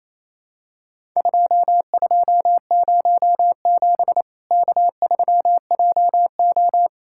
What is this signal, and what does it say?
Morse code, heard as a keyed tone, about 28 wpm, 700 Hz. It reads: 2207 K3JO
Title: Worked All Europe DX Contest - QTC Exchange - 28wpm